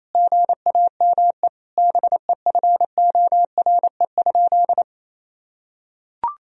Morse code, heard as a keyed tone, about 28 wpm, 700 Hz. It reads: GAME BEFORE?
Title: Sentences from Top 1000 Words - Review Part 3 - 28wpm